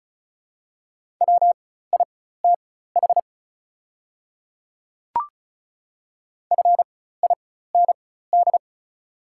Morse code, heard as a keyed tone, about 36 wpm, spaced as for 16 wpm, 700 Hz. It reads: WITH FIND
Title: 4x Character Spacing - Top 100 Words - 35wpm